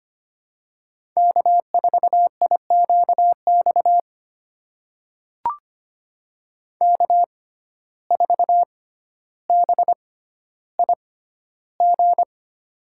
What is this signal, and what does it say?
Morse code, heard as a keyed tone, about 25 wpm, spaced as for 13 wpm, 700 Hz. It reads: K4IQX K4BIG
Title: USA Call Signs - 5 Characters - 6x Character Spacing - 25wpm